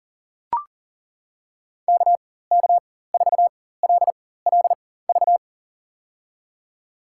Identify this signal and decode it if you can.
Morse code, heard as a keyed tone, about 40 wpm, spaced as for 19 wpm, 700 Hz. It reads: KK4LLV
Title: USA Call Signs - 6 Characters - 4x Character Spacing - 40wpm